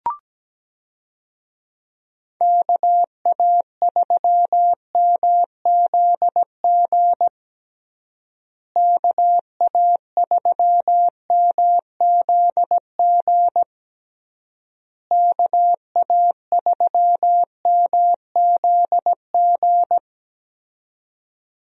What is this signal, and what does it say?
Morse code, heard as a keyed tone, about 17 wpm, 700 Hz. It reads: KA3MZG KA3MZG KA3MZG
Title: USA Call Signs - 6 Characters - Repeated 3x - 17wpm